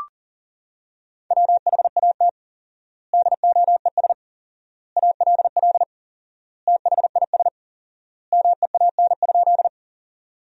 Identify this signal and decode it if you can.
Morse code, heard as a keyed tone, about 40 wpm, 700 Hz. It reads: WHAT DOES ALL THIS MEAN?
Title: 4x Word-Spacing - Sentences from Top 200 Words - 40wpm